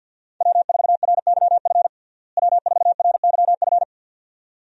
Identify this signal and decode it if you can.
Morse code, heard as a keyed tone, about 52 wpm, 700 Hz. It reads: W4RYF W4RYF
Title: USA Call Signs - 5 Characters - Repeated 2x - 50wpm